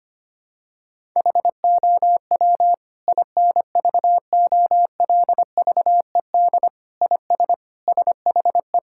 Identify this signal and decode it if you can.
Morse code, heard as a keyed tone, about 25 wpm, 700 Hz. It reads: HOW INVOLVED IS SHE
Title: Sentences from Top 1000 Words - Review Part 5 - 25wpm